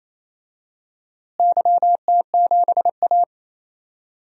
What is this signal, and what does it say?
Morse code, heard as a keyed tone, about 28 wpm, 700 Hz. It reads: YT7A
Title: DXCC-100 Call Signs - 4 Characters - Repeated 3x - 28wpm